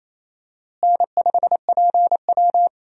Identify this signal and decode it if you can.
Morse code, heard as a keyed tone, about 28 wpm, 700 Hz. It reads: N5PW
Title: USA Call Signs - 4 Characters - Repeated 2x - 28wpm